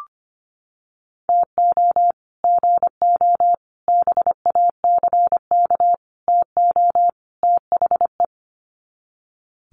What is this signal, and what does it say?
Morse code, heard as a keyed tone, about 25 wpm, 700 Hz. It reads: TO GO BACK TO THE
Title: Sets of 5 Words 25wpm